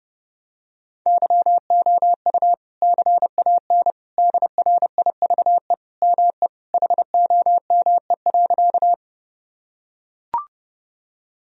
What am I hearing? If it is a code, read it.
Morse code, heard as a keyed tone, about 32 wpm, 700 Hz. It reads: YOU CAN DRIVE ME HOME.
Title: Sentences from Top 1000 Words - Review Part 2 - 30wpm